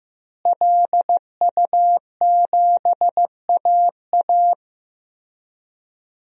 Morse code, heard as a keyed tone, about 15 wpm, 700 Hz. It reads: LU7AA
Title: DXCC-100 Call Signs - 5 Characters - Repeated 3x - 15wpm